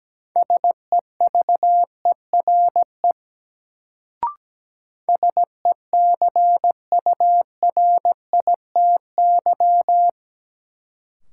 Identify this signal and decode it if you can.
Morse code, heard as a keyed tone, about 17 wpm, 700 Hz. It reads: SEVERE SECURITY
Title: Sentences from Top 1400 Words - New Word Drill Alt - 17wpm